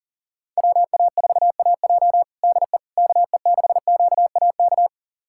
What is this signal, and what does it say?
Morse code, heard as a keyed tone, about 40 wpm, 700 Hz. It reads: WA4AJ DE KE6QAK
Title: USA Callsign DE Callsign 40wpm